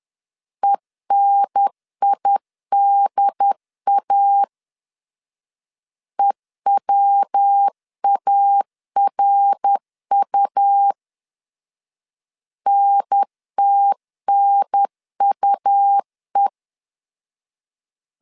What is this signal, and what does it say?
Morse code, heard as a keyed tone, about 10 wpm, 800 Hz. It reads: ENIDA EWARU NTNUE
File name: CW / DET / 1 3 1500C S C 10wpm 3335